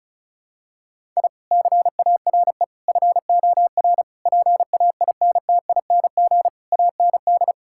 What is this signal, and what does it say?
Morse code, heard as a keyed tone, about 36 wpm, 700 Hz. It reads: I CARE FOR PAINTING AND
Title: Sentences from Top 1000 Words - Review Part 6 - 35wpm